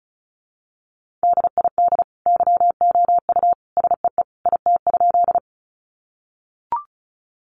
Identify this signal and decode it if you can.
Morse code, heard as a keyed tone, about 34 wpm, 700 Hz. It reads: DID YOU SEE IT?
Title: Sentences from Top 100 Words 35wpm